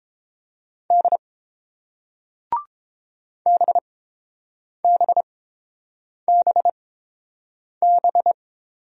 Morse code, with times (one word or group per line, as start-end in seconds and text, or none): 0.9-1.2 D
3.5-8.3 BBBB